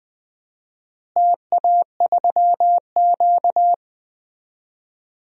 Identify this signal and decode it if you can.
Morse code, heard as a keyed tone, about 21 wpm, 700 Hz. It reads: TA3Q
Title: DXCC-100 Call Signs - 4 Characters - Repeated 3x - 20wpm